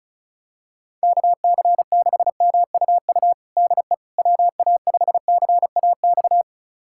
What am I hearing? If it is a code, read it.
Morse code, heard as a keyed tone, about 36 wpm, 690 Hz. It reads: KC6MUU DE WA5CAX
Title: USA Callsign DE Callsign - 6 Characters - 35wpm